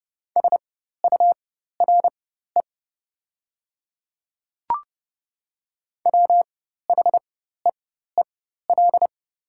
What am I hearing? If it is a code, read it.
Morse code, heard as a keyed tone, about 30 wpm, spaced as for 14 wpm, 690 Hz. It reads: SURE WHEEL
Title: 4x Character Spacing - Top 500 Words - 30wpm